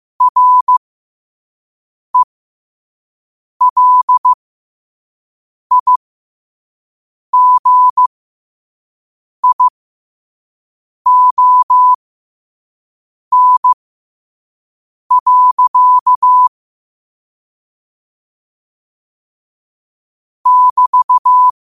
Morse code, heard as a keyed tone, about 15 wpm, spaced as for 5 wpm, 1000 Hz. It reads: RELIGION. <BT>